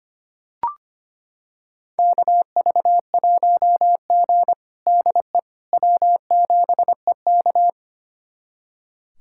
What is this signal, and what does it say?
Morse code, heard as a keyed tone, about 25 wpm, 700 Hz. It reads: KV1G DE W7EK
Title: USA Callsign DE Callsign - 4 Characters - 25wpm